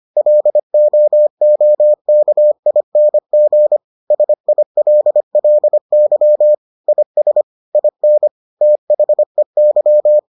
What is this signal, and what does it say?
Morse code, heard as a keyed tone, about 25 wpm, 600 Hz. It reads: LOOKING SILLY IS IN THEY